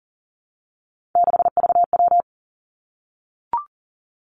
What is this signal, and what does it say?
Morse code, heard as a keyed tone, about 39 wpm, 700 Hz. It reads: 6VW
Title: Three Challenging Letter-Number Combinations 40wpm